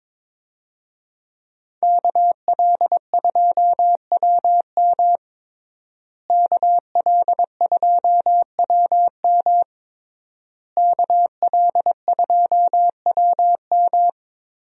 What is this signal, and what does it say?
Morse code, heard as a keyed tone, about 22 wpm, 700 Hz. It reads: KL2WM KL2WM KL2WM
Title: USA Call Signs - 5 Characters - Repeated 3x - 22wpm